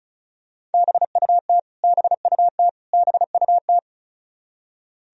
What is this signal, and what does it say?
Morse code, heard as a keyed tone, about 36 wpm, 700 Hz. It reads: BUT BUT BUT
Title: Top 100 Words Repeated 35wpm